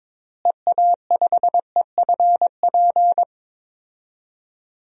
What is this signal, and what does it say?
Morse code, heard as a keyed tone, about 22 wpm, 700 Hz. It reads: EA5EFP
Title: DXCC-100 Call Signs - 6 Characters - Repeated 2x - 22wpm